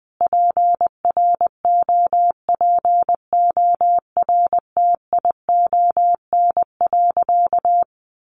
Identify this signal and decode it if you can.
Morse code, heard as a keyed tone, about 20 wpm, 700 Hz. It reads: PROPORTION.
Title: A Sound Of Thunder 20wpm